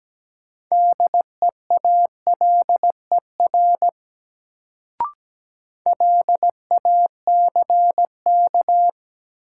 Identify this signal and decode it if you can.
Morse code, heard as a keyed tone, about 17 wpm, 700 Hz. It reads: DEALER LACK
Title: Sentences from Top 1100 Words - New Word Drill - 17wpm